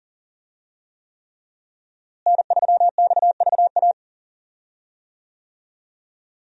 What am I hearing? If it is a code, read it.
Morse code, heard as a keyed tone, about 40 wpm, 700 Hz. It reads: N3XVA
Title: USA Call Signs - 5 Characters - Repeated 3x - 40wpm